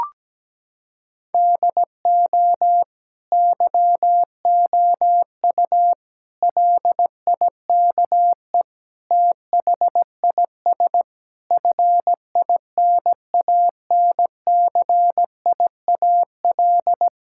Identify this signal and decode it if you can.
Morse code, heard as a keyed tone, about 17 wpm, 700 Hz. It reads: DO YOU LIKE THIS FINANCIAL
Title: Sentences from Top 1000 Words - Review Part 5 - 17wpm